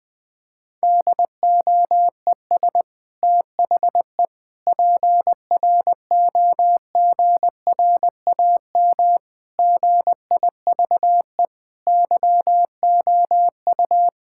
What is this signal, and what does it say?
Morse code, heard as a keyed tone, about 20 wpm, 700 Hz. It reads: DOES THE PROGRAM GIVE YOU
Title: Sentences from Top 1000 Words - Review Part 5 - 20wpm